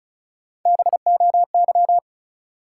Morse code, heard as a keyed tone, about 34 wpm, 700 Hz. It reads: BOY
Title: Three Letter Words - 35wpm